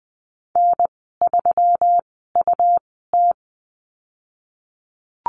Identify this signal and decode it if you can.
Morse code, heard as a keyed tone, about 20 wpm, spaced as for 14 wpm, 700 Hz. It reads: N3UT
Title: 122 Lesson 20g - Number 3 - Call Signs 20wpm